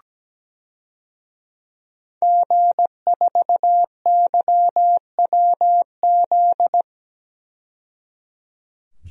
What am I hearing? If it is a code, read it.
Morse code, heard as a keyed tone, about 17 wpm, 700 Hz. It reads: G4YWZ
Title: DXCC-100 Call Signs - 5 Characters - 17wpm